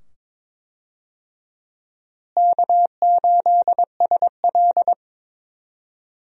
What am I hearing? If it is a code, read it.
Morse code, heard as a keyed tone, about 22 wpm, 700 Hz. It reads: K8SL
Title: USA Call Signs - 4 Characters - Repeated 2x - 22wpm